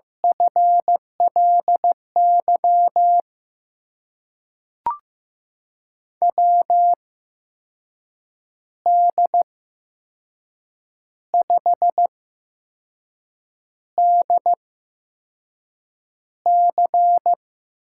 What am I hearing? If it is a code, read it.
Morse code, heard as a keyed tone, about 15 wpm, spaced as for 5 wpm, 700 Hz. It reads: FLY WD5DC
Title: USA Call Signs - 6 Characters - 8x Character Spacing - 15wpm